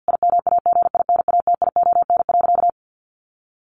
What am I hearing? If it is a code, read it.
Morse code, heard as a keyed tone, about 50 wpm, 700 Hz. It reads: IMAGINATION.
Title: Sentences from Top 1300 Words 50wpm